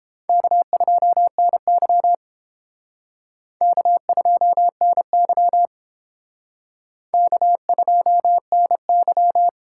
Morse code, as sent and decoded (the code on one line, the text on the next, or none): -.- ..--- -. -.-- / -.- ..--- -. -.-- / -.- ..--- -. -.--
K2NY K2NY K2NY